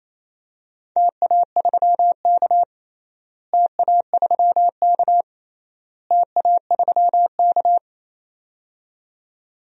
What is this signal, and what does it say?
Morse code, heard as a keyed tone, about 28 wpm, 700 Hz. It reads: TA3K TA3K TA3K
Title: DXCC-100 Call Signs - 4 Characters - Repeated 3x - 28wpm